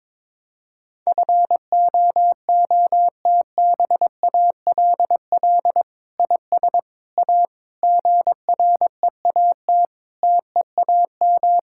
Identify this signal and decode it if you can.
Morse code, heard as a keyed tone, about 22 wpm, 700 Hz. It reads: FOOTBALL IS A GREAT TEAM
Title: Sentences from Top 1600 Words 22wpm